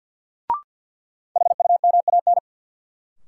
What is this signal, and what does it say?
Morse code, heard as a keyed tone, about 52 wpm, 700 Hz. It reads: HUMAN
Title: Sentences from Top 600 Words - New Word Drill - 50wpm